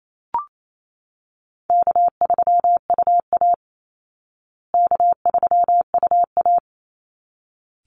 Morse code, heard as a keyed tone, about 28 wpm, 700 Hz. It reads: K3UA K3UA
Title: Easy Call Signs Repeated 2x - 28wpm